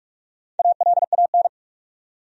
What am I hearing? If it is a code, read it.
Morse code, heard as a keyed tone, about 43 wpm, 700 Hz. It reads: ALAN
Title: USA Common Names - Max 4 Letters - 45wpm